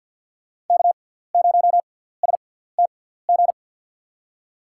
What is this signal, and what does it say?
Morse code, heard as a keyed tone, about 50 wpm, spaced as for 17 wpm, 700 Hz. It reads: K0STG
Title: USA Call Signs - 5 Characters - 6x Character Spacing - 50wpm